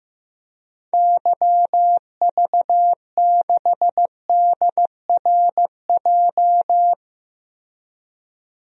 Morse code, heard as a keyed tone, about 15 wpm, 700 Hz. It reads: YV6DRJ
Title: DXCC-100 Call Signs - 6 Characters - Repeated 2x - 15wpm